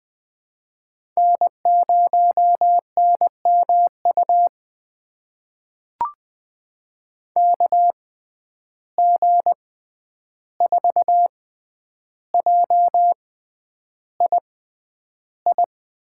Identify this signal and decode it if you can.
Morse code, heard as a keyed tone, about 20 wpm, spaced as for 10 wpm, 700 Hz. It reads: N0NMU KG4JII
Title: USA Call Signs - 6x Character Spacing - 20wpm